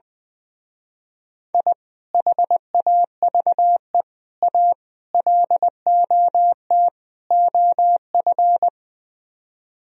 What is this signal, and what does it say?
Morse code, heard as a keyed tone, about 20 wpm, 700 Hz. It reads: I HAVE A LOT OF
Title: Speed-Racing - Sets of 5 Words 20wpm